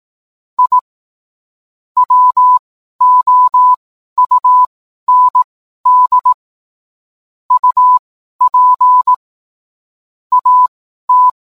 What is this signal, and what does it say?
Morse code, heard as a keyed tone, about 18 wpm, spaced as for 12 wpm, 1000 Hz. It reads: I WOUND UP AT
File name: cwblog / cw mp3 directory / mem morse 12